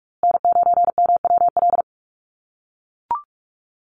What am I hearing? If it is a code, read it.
Morse code, heard as a keyed tone, about 44 wpm, 700 Hz. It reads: N9MWL
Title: USA Call Signs Repeated 2x 45wpm